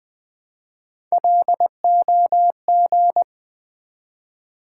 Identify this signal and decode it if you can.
Morse code, heard as a keyed tone, about 20 wpm, 690 Hz. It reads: LOG
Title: Build A Word - 1 Letter At A Time - 3-Letter Words - 20wpm